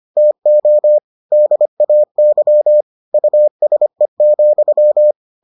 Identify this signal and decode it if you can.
Morse code, heard as a keyed tone, about 25 wpm, 600 Hz. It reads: TO DAY USE,